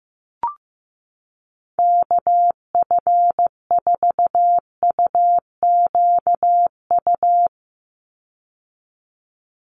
Morse code, heard as a keyed tone, about 15 wpm, 700 Hz. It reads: KF4UQU
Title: USA Call Signs 15wpm